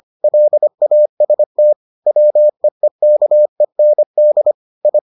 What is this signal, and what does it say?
Morse code, heard as a keyed tone, about 25 wpm, 600 Hz. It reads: LAST WEEKEND I